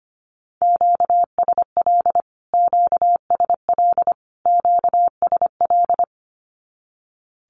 Morse code, heard as a keyed tone, about 25 wpm, 700 Hz. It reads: QSL QSL QSL
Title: Q-Codes Repeated 3x 25wpm